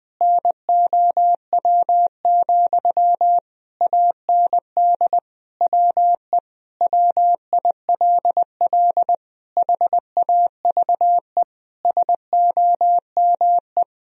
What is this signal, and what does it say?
Morse code, heard as a keyed tone, about 20 wpm, 700 Hz. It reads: NOW, AND WE WILL HAVE SOME
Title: Sentences from Top 1000 Words - Review Part 4 - 20wpm